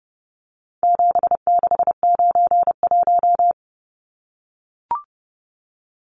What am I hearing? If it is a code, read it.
Morse code, heard as a keyed tone, about 30 wpm, 700 Hz. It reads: 7691
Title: Numbers - 4-Digit 30wpm